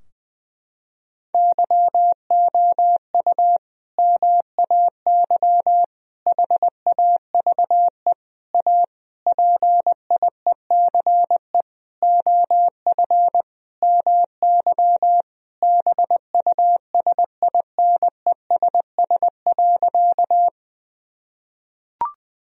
Morse code, heard as a keyed tone, about 21 wpm, 700 Hz. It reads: YOU MAY HAVE A PIECE OF MY BUSINESS.
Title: Sentences from Top 1000 Words - Review Part 2 - 20wpm